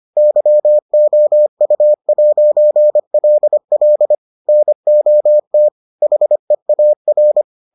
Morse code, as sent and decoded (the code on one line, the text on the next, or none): -.-- --- ..- .----. .-.. .-.. / -. --- - / .... .- .-.
YOU'LL NOT HEAR